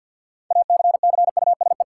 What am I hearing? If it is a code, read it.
Morse code, heard as a keyed tone, about 52 wpm, 700 Hz. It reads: ACCUSE
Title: Six Letter Words - 50wpm